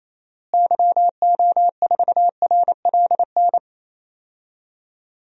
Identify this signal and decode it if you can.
Morse code, heard as a keyed tone, about 28 wpm, 700 Hz. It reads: YO4RLN
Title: DXCC-100 Call Signs - 6 Characters - Repeated 2x - 28wpm